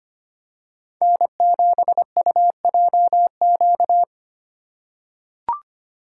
Morse code, heard as a keyed tone, about 25 wpm, 700 Hz. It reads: N7UJQ
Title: USA Call Signs - 5 Characters - 6x Character Spacing - 25wpm